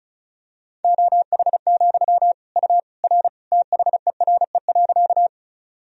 Morse code, heard as a keyed tone, about 34 wpm, 700 Hz. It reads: OH, U R THERE.
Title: Sentences from Top 500 Words and Abbreviations 35wpm